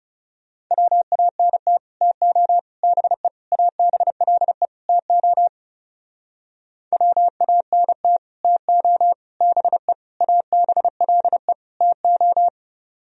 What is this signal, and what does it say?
Morse code, heard as a keyed tone, about 36 wpm, 700 Hz. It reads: WANT TO BE ABLE TO WANT TO BE ABLE TO
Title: Speed-Racing - Sets of 5 Words 30wpm